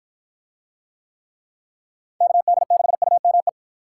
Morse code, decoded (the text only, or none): KD6UME